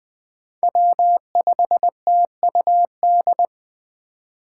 Morse code, heard as a keyed tone, about 20 wpm, 700 Hz. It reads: W5TUD
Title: Speed Racing - USA 1x3 Call Signs 17wpm